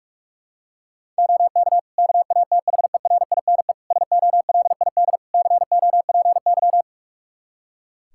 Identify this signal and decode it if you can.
Morse code, heard as a keyed tone, about 42 wpm, 700 Hz. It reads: OK KATHERINE SOLID COPY
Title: Practice QSO - Exchange 3 - More Info - 45wpm